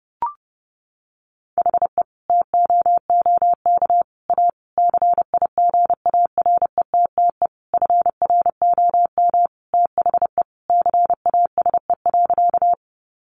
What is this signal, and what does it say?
Morse code, heard as a keyed tone, about 30 wpm, 700 Hz. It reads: HE TOOK A CIGARETTE FROM THE CASE.
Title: Sentences from Top 1000 Words 30wpm